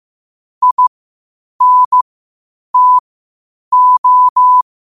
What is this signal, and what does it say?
Morse code, heard as a keyed tone, about 15 wpm, spaced as for 8 wpm, 1000 Hz. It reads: INTO
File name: qotdfiles / 8wpm